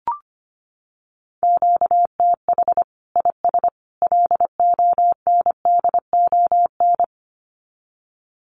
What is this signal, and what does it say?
Morse code, heard as a keyed tone, about 25 wpm, 700 Hz. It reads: QTH IS LONDON
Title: Common QSO Phrases 25wpm